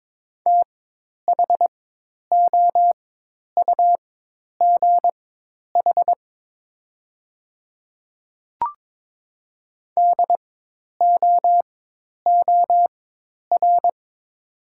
Morse code, heard as a keyed tone, about 22 wpm, spaced as for 10 wpm, 700 Hz. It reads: THOUGH DOOR
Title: 4x Character Spacing - Top 500 Words - 22wpm